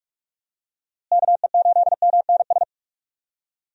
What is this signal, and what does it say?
Morse code, heard as a keyed tone, about 44 wpm, 700 Hz. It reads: KE8MNS
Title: USA Call Signs - 6 Characters - Repeated 3x - 45wpm